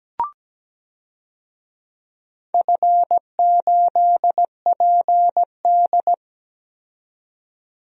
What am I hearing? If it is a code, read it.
Morse code, heard as a keyed tone, about 17 wpm, 700 Hz. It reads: F8PD
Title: DXCC-100 Call Signs - 4 Characters - 17wpm